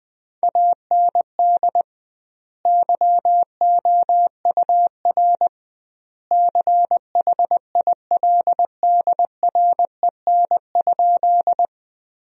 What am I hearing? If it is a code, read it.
Morse code, heard as a keyed tone, about 20 wpm, 700 Hz. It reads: AND YOUR CHILDREN?
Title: 2x Word-Spacing - Sentences from Top 300 Words - 20wpm